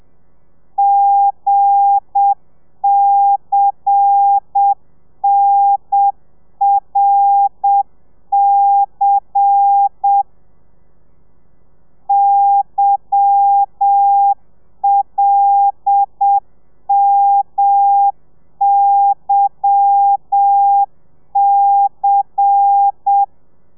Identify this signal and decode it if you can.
Morse code, heard as a keyed tone, about 7 wpm, 800 Hz. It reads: GCNRC YLMYC
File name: telegrafia / i2tfj / Lez10 30